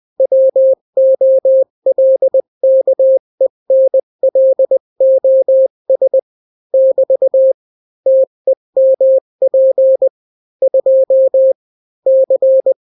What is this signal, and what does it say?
Morse code, heard as a keyed tone, about 20 wpm, 530 Hz. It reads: WOLKENLOS <BT> TEMP 2 C
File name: daily / 18wpm